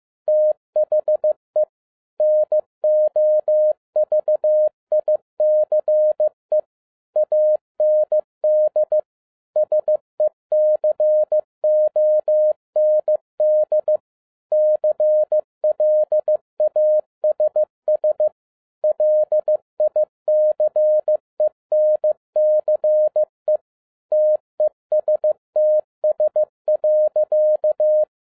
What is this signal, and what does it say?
Morse code, heard as a keyed tone, about 15 wpm, 610 Hz. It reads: THE NOVICE AND SECOND CLASS LICENCE TESTS.